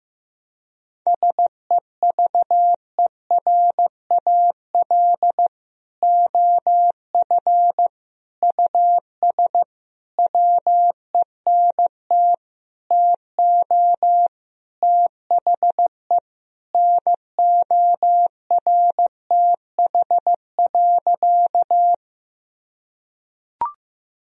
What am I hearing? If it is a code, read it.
Morse code, heard as a keyed tone, about 15 wpm, 700 Hz. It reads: SEVERAL OF US WENT TO THE NORTH.
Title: Sentences from Top 1000 Words - Review Part 2 - 15wpm